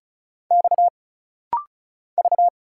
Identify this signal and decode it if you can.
Morse code, heard as a keyed tone, about 37 wpm, 700 Hz. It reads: X V
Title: Single Letters - Rapid-Fire 35wpm